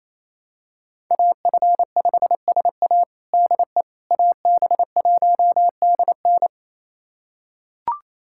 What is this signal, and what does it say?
Morse code, heard as a keyed tone, about 28 wpm, 700 Hz. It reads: AF5SA DE AB1DN
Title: USA Callsign DE Callsign - 5 Characters - 28wpm